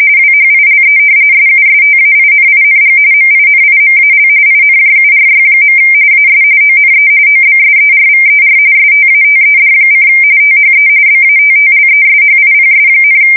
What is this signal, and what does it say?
RTTY, about 45.45 baud. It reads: WHILE SUSTAINING MINIMAL INJURY. AS ONE OBSERVER STATED: THE BATTLESHIPS,
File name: fsk170 / battleshipnj